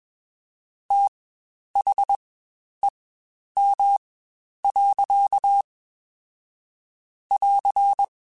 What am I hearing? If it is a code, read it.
Morse code, heard as a keyed tone, about 21 wpm, spaced as for 10 wpm, 800 Hz. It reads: THEM. <AR>
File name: Spaced / Chuckles Spaced-mp3s / SoftWords sp @20